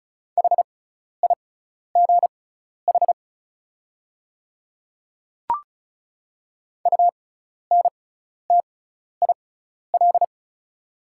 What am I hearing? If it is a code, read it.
Morse code, heard as a keyed tone, about 36 wpm, spaced as for 12 wpm, 700 Hz. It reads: HIGH UNTIL
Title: 6x Character Spacing - Top 500 Words - 35wpm